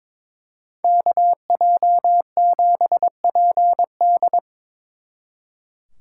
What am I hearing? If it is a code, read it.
Morse code, heard as a keyed tone, about 26 wpm, 700 Hz. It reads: KJ7PD
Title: Speed Racing - USA 2x2 Call Signs 22wpm